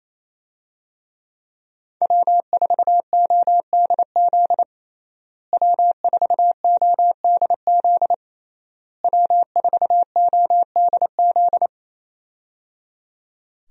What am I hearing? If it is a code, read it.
Morse code, heard as a keyed tone, about 28 wpm, 700 Hz. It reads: W4ODZ W4ODZ W4ODZ
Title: USA Call Signs - 5 Characters - Repeated 3x - 28wpm